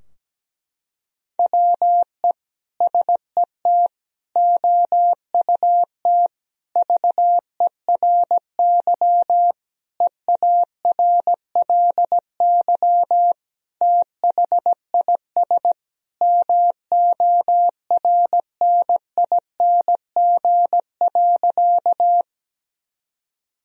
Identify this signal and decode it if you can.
Morse code, heard as a keyed tone, about 17 wpm, 700 Hz. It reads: WE SET OUT VERY EARLY THIS MORNING.
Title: Sentences from Top 1000 Words - Review Part 2 - 17wpm